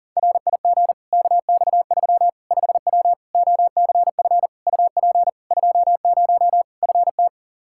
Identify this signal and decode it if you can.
Morse code, heard as a keyed tone, about 40 wpm, 700 Hz. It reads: RIG KX3 5W OCF UP 20 FT
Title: Practice QSO - Exchange 2 - Info - 40wpm